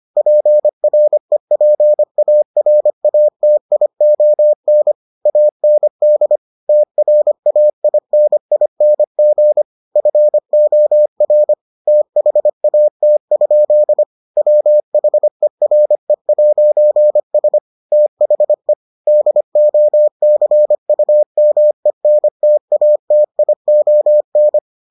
Morse code, heard as a keyed tone, about 25 wpm, 600 Hz. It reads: PREPARATION AND TRAINING FOR THAT? WHERE'S THE DOCUMENTATION